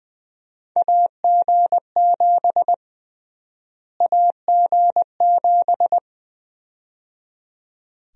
Morse code, heard as a keyed tone, about 20 wpm, 700 Hz. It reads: AG7 AG7
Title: USA Call Sign Prefixes - Repeated 2x - 20wpm